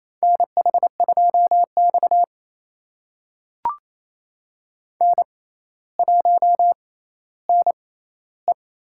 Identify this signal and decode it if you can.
Morse code, heard as a keyed tone, about 29 wpm, spaced as for 15 wpm, 700 Hz. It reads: NH2X N1NE
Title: USA Call Signs - 4 Characters - 6x Character Spacing - 28wpm